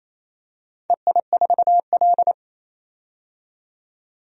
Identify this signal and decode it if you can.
Morse code, heard as a keyed tone, about 28 wpm, 700 Hz. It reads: EI4L
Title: DXCC-100 Call Signs - 4 Characters - 28wpm